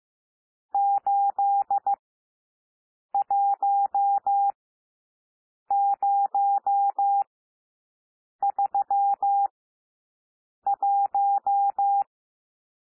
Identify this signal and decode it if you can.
Morse code, heard as a keyed tone, about 15 wpm, spaced as for 6 wpm, 800 Hz. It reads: 81031
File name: cwmp3 / N1 SOUND NUMBERS 15 wpm